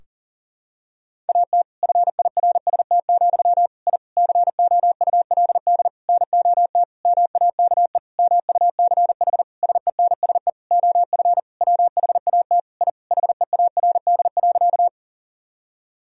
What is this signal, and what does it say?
Morse code, heard as a keyed tone, about 40 wpm, 700 Hz. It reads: AT FIRST, I COULD NOT MAKE MUCH SENSE OF WHAT I HEARD.
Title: Sentences from Top 1000 Words - Review Part 4 - 40wpm